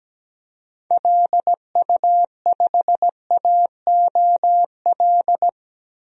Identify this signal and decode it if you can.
Morse code, heard as a keyed tone, about 17 wpm, 700 Hz. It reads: LU5AOL
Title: DXCC-100 Call Signs - 6 Characters - 17wpm